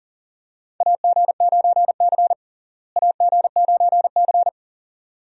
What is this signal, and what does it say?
Morse code, heard as a keyed tone, about 37 wpm, 700 Hz. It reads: AG9C AG9C
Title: USA Call Signs - 4 Characters - Repeated 2x - 40wpm